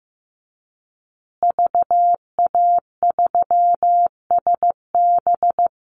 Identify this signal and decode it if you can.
Morse code, heard as a keyed tone, about 15 wpm, 700 Hz. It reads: VA3SB
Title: Call Signs Repeated 2x - 15wpm